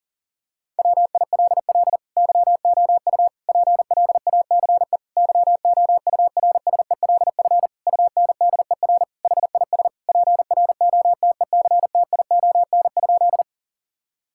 Morse code, heard as a keyed tone, about 40 wpm, 700 Hz. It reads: WILL YOU PLACE YOURSELF UNDER HIS PROTECTION?
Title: Sentences from Top 1600 Words 40wpm